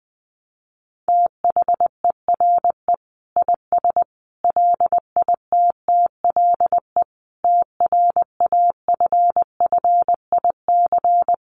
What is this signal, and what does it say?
Morse code, heard as a keyed tone, about 20 wpm, 700 Hz. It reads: THERE IS LITTLE TRAFFIC
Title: Sentences from Top 1300 Words 20wpm